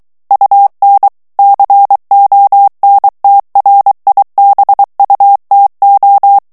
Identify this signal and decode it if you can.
Morse code, heard as a keyed tone, about 23 wpm, 800 Hz. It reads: UN CONTRIBUTO
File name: ITALIEN texte clair 1 vitesse 22 mots minute